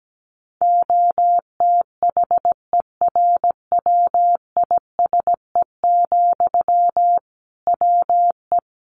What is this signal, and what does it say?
Morse code, heard as a keyed tone, about 17 wpm, 700 Hz. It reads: OTHERWISE, WE
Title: Sentences from Top 1500 Words 17wpm